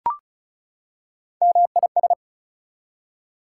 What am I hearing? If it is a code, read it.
Morse code, heard as a keyed tone, about 34 wpm, 700 Hz. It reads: MIS
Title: Prefixes and Suffixes 35wpm